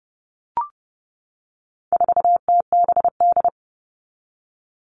Morse code, heard as a keyed tone, about 30 wpm, 700 Hz. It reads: 4TBD
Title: All Cut Numbers - 4-Digit 30wpm